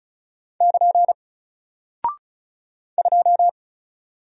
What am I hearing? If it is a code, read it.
Morse code, heard as a keyed tone, about 34 wpm, 700 Hz. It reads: <KN> 2
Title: Single Character (Letter:Number:Punctuation:Prosign) Alt 35wpm